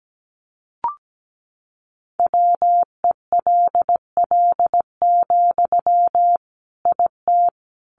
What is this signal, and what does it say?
Morse code, heard as a keyed tone, about 17 wpm, 700 Hz. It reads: WELL, IT
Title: Sentences from Top 1500 Words 17wpm